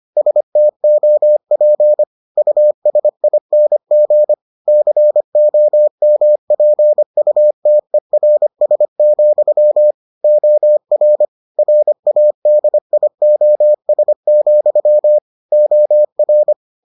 Morse code, heard as a keyed tone, about 25 wpm, 600 Hz. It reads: STOP USING COMPUTERS, OR RADIOS, OR